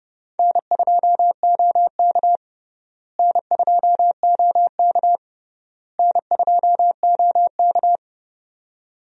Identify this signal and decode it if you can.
Morse code, heard as a keyed tone, about 32 wpm, 700 Hz. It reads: N2OK N2OK N2OK
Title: USA Call Signs - 4 Characters - Repeated 3x - 30wpm